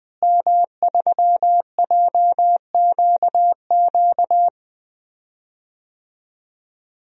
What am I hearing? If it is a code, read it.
Morse code, heard as a keyed tone, about 20 wpm, 700 Hz. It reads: M3JQQ
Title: DXCC-100 Call Signs - 5 Characters - Repeated 3x - 20wpm